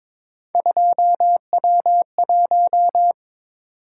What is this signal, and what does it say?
Morse code, heard as a keyed tone, about 26 wpm, 700 Hz. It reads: 2W1
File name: Speed-Racing - Three Letter-Number Combinations - 22wpm